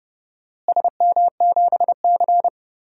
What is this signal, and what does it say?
Morse code, heard as a keyed tone, about 29 wpm, 700 Hz. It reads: SM7C
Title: DXCC-100 Call Signs - 4 Characters - Repeated 3x - 30wpm